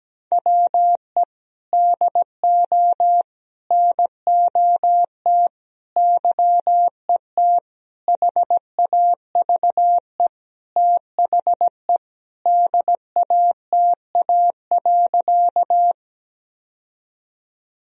Morse code, heard as a keyed tone, about 17 wpm, 700 Hz. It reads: WE DO NOT YET HAVE THE DATA.
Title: Sentences from Top 1000 Words - Review Part 3 - 17wpm